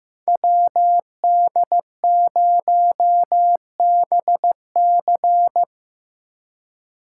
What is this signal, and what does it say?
Morse code, heard as a keyed tone, about 17 wpm, 700 Hz. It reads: WD0BC
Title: Speed Racing - USA 2x2 Call Signs 15wpm